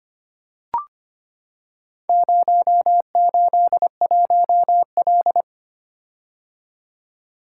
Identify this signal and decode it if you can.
Morse code, heard as a keyed tone, about 25 wpm, 690 Hz. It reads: 081L
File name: practice / 25wpm / Four Letter-Number Combinations - 25wpm